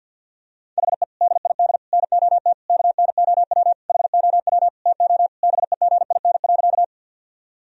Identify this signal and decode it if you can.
Morse code, heard as a keyed tone, about 50 wpm, 700 Hz. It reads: HE DID NOT KNOW HOW TO BEGIN.
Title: Sentences from Top 1000 Words - Review Part 1 - 50wpm